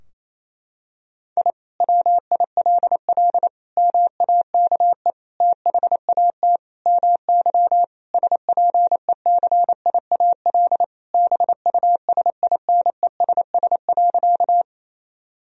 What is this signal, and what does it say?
Morse code, heard as a keyed tone, about 28 wpm, 700 Hz. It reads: I WILL MAKE THAT MY SPECIAL BUSINESS.
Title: Sentences from Top 1000 Words - Review Part 3 - 28wpm